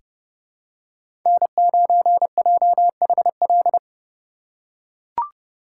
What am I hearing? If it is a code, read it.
Morse code, heard as a keyed tone, about 30 wpm, 700 Hz. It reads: N9JHL
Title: K1USN SST QSO Practice - 30wpm